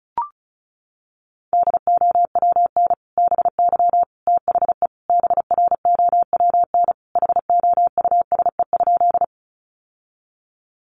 Morse code, heard as a keyed tone, about 34 wpm, 700 Hz. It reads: DOWN BY THE BROWN HOUSE?
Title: Sentences from Top 1200 Words 35wpm